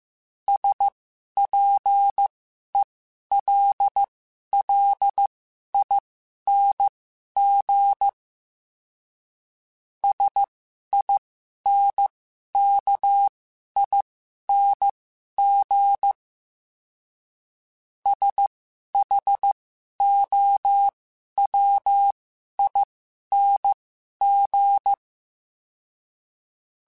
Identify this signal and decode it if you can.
Morse code, heard as a keyed tone, about 15 wpm, spaced as for 11 wpm, 800 Hz. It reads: SPELLING SINKING SHOWING
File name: Special INGs-mp3s / Special INGs-mp3s / Sngl Ing @10